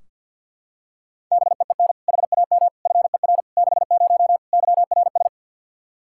Morse code, heard as a keyed tone, about 45 wpm, 700 Hz. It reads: BEEN HAM FER 60 YRS
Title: Practice QSO - Exchange 3 - More Info - 50wpm